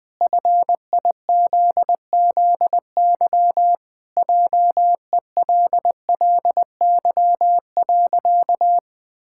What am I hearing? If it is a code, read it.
Morse code, heard as a keyed tone, about 20 wpm, 700 Hz. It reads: FIZZY JELLY.
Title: Pangram Challenge v2 - 20wpm